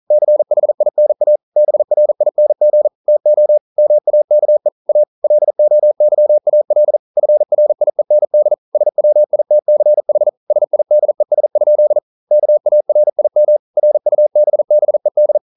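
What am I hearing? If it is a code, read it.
Morse code, heard as a keyed tone, about 41 wpm, 600 Hz. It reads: CHINA BRING TO MAKE A LOYAL FRIEND SWITCH SIDES? KARIM RUBBED
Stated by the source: Morse code at 40 wpm, practice text, CW med Gnister.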